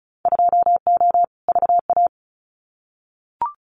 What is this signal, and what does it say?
Morse code, heard as a keyed tone, about 36 wpm, 700 Hz. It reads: 2O VA
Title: Winter Field Day Exchanges 35wpm